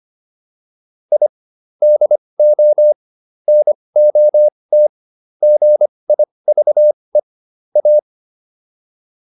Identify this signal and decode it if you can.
Morse code, heard as a keyed tone, about 25 wpm, spaced as for 20 wpm, 600 Hz. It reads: I DO NOT GIVE A